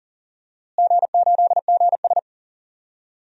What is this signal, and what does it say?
Morse code, heard as a keyed tone, about 40 wpm, 700 Hz. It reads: G8GS
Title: DXCC-100 Call Signs - 4 Characters - Repeated 2x - 40wpm